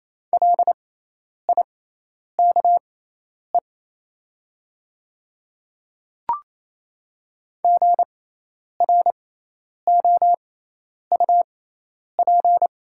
Morse code, heard as a keyed tone, about 28 wpm, spaced as for 10 wpm, 700 Hz. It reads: LIKE GROUP
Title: 6x Character Spacing - Top 300 Words - 28wpm